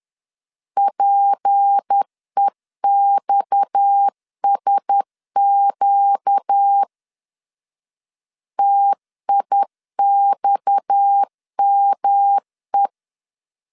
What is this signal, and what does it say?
Morse code, heard as a keyed tone, about 10 wpm, 800 Hz. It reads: PEXSQ TIXME
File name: LEZIONI / 6 PKX0 / 6 PKX 10wpm 2426